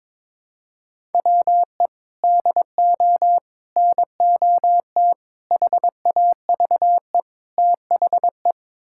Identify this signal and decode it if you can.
Morse code, heard as a keyed tone, about 21 wpm, 700 Hz. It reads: WE DO NOT HAVE THE
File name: practice / 22wpm / Build A Phrase - 5-Word Phrases - 22wpm